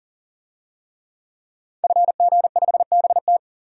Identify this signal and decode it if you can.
Morse code, heard as a keyed tone, about 40 wpm, 700 Hz. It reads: FG5BT
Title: DXCC-100 Call Signs - 5 Characters - 40wpm